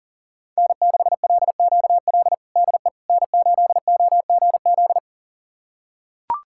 Morse code, heard as a keyed tone, about 40 wpm, 700 Hz. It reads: N6LQL DE N8OGZ
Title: USA Callsign DE Callsign - 5 Characters - 40wpm